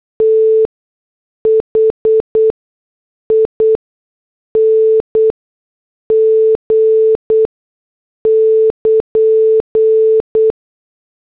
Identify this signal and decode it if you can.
Morse code, heard as a keyed tone, about 8 wpm, spaced as for 6 wpm, 440 Hz. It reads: THING <KN>
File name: morse / headlines